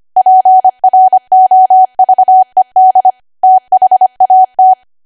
Morse code, heard as a keyed tone, about 25 wpm, 750 Hz. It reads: PROVED THAT
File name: Archive / 25 WPM / 25WPM